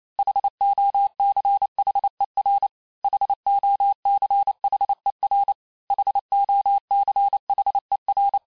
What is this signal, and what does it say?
Morse code, heard as a keyed tone, about 28 wpm, 800 Hz. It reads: HOCHER HOCHER HOCHER
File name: cours-du-mercredi / audios / cours-865-teste-les-mots-vitesse-27mpm